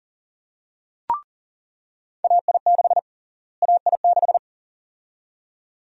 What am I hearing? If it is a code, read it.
Morse code, heard as a keyed tone, about 40 wpm, 700 Hz. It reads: AI6 AI6
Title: USA Call Sign Prefixes - Repeated 2x - 40wpm